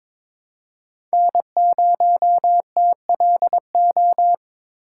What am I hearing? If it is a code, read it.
Morse code, heard as a keyed tone, about 21 wpm, 700 Hz. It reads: N0TLO
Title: USA Call Signs - 6x Character Spacing - 22wpm